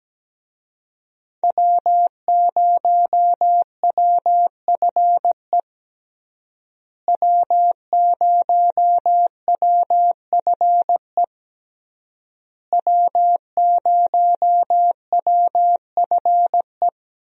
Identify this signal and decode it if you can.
Morse code, heard as a keyed tone, about 17 wpm, 700 Hz. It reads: W0WFE W0WFE W0WFE
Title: USA Call Signs - 5 Characters - Repeated 3x - 17wpm